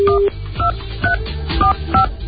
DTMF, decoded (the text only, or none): 12345